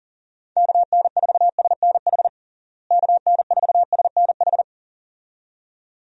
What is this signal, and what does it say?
Morse code, heard as a keyed tone, about 40 wpm, 700 Hz. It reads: KN4SNH KN4SNH
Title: USA Call Signs - 6 Characters - Repeated 2x - 40wpm